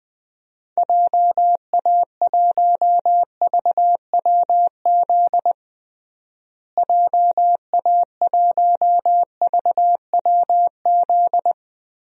Morse code, heard as a keyed tone, about 20 wpm, 700 Hz. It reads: JA1VWZ JA1VWZ
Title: DXCC-100 Call Signs - 6 Characters - Repeated 3x - 20wpm